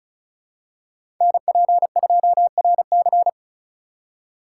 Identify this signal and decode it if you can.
Morse code, heard as a keyed tone, about 34 wpm, 700 Hz. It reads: NP2RC
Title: USA Call Signs - 5 Characters - 2x Character Spacing - 35wpm